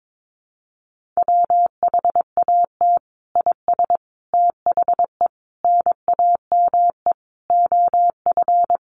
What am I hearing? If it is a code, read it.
Morse code, heard as a keyed tone, about 22 wpm, 700 Hz. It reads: WHAT IS THE NAME OF
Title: Sentences from Top 1500 Words 22wpm